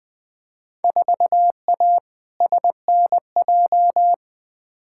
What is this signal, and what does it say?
Morse code, heard as a keyed tone, about 20 wpm, 700 Hz. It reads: 4A SNJ
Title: Field Day Exchanges Encore - 20wpm